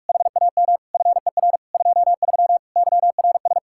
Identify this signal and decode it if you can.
Morse code, heard as a keyed tone, about 45 wpm, 700 Hz. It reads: HAM FER 23 YRS